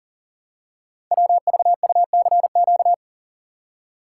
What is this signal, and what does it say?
Morse code, heard as a keyed tone, about 38 wpm, 690 Hz. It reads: WVUCQ
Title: Code Groups - 5 Letters - 40wpm